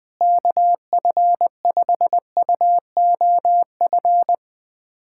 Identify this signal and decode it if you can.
Morse code, heard as a keyed tone, about 20 wpm, 700 Hz. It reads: KF5UOF
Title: USA Callsign DE Callsign - 6 Characters - 20wpm